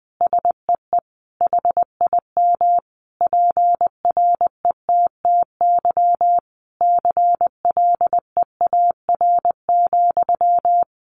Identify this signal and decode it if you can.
Morse code, heard as a keyed tone, about 20 wpm, 700 Hz. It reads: SEE HIM PRETTY CLEAR,
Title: Sentences from Top 1000 Words 20wpm